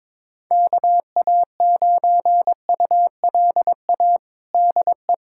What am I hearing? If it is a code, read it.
Morse code, heard as a keyed tone, about 22 wpm, 700 Hz. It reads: KA9ULA DE
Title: USA Callsign DE Callsign - 6 Characters - 22wpm